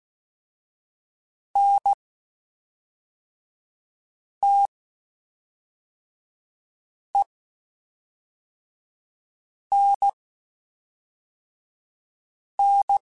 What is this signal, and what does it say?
Morse code, heard as a keyed tone, about 16 wpm, spaced as for 3 wpm, 800 Hz. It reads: NTENN
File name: (spaced) / QSOs Spaced-mp3s / kc7sac @5 sp